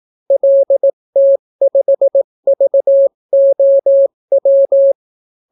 Morse code, heard as a keyed tone, about 18 wpm, 550 Hz. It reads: LT5VOW